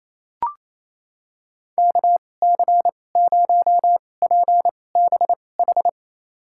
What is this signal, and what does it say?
Morse code, heard as a keyed tone, about 28 wpm, spaced as for 20 wpm, 700 Hz. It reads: KC0PBH
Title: USA Call Signs - 6 Characters - 2x Character Spacing - 28wpm